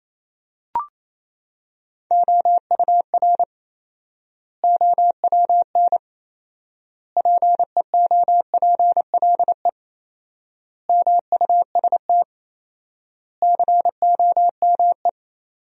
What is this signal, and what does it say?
Morse code, heard as a keyed tone, about 28 wpm, 700 Hz. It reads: OUR OWN PEOPLE MUST COME
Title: 4x Word-Spacing - Sentences from Top 300 Words - 28wpm